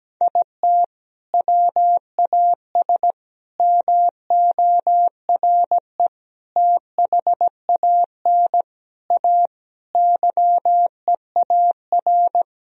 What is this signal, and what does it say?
Morse code, heard as a keyed tone, about 17 wpm, 700 Hz. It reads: IT WAS MORE THAN A YEAR
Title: Sentences from Top 1000 Words - Review Part 3 - 17wpm